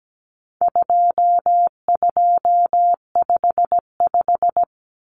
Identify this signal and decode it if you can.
Morse code, heard as a keyed tone, about 17 wpm, 700 Hz. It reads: 2255
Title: Numbers - 4-Digit 17wpm